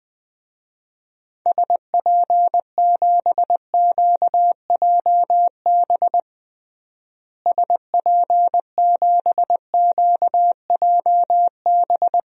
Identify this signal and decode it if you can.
Morse code, heard as a keyed tone, about 20 wpm, 700 Hz. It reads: SP7QJB SP7QJB
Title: DXCC-100 Call Signs - 6 Characters - Repeated 3x - 20wpm